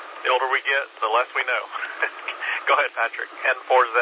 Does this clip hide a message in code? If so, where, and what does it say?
Yes: Emergency Alert System (SAME) tones from 0.5 to 3.3 s, heard as harsh buzzing bursts, with speech.